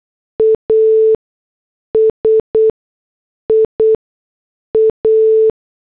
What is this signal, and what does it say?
Morse code, heard as a keyed tone, about 8 wpm, spaced as for 6 wpm, 440 Hz. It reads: ASIA